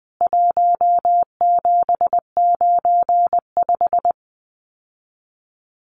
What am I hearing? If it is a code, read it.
Morse code, heard as a keyed tone, about 20 wpm, 700 Hz. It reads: 1795
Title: CWT Exchange 20wpm